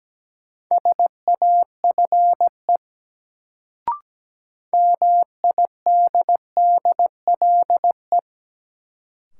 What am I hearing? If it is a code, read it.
Morse code, heard as a keyed tone, about 17 wpm, 700 Hz. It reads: SAFE MIDDLE
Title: Sentences from Top 900 Words - New Word Drill - 17wpm